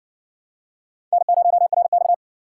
Morse code, heard as a keyed tone, about 53 wpm, 700 Hz. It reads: N0RX